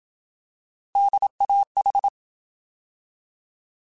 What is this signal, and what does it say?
Morse code, heard as a keyed tone, about 27 wpm, 800 Hz. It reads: DAH
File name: Alphabetical / Alphabetical mp3s / Ds @25